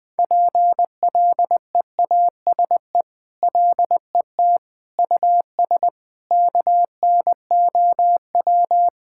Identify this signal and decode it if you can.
Morse code, heard as a keyed tone, about 19 wpm, 700 Hz. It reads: PLEASE LET US KNOW
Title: Sentences with 6-Word Phrases - 20wpm